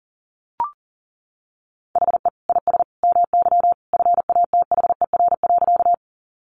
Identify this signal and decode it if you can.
Morse code, heard as a keyed tone, about 40 wpm, 700 Hz. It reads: HE IS MY FATHER.
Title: Sentences from Top 300 Words 40wpm